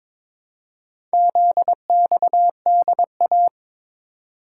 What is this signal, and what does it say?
Morse code, heard as a keyed tone, about 22 wpm, 690 Hz. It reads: ZXDA
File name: practice / 22wpm / Four Letter-Number Combinations - 22wpm